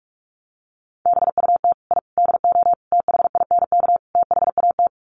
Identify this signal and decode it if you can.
Morse code, heard as a keyed tone, about 44 wpm, 690 Hz. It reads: BUT I DO THINK THAT
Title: Sets of 5 Words Encore2 45wpm